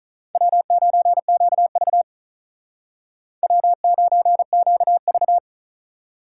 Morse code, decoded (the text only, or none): W9QV W9QV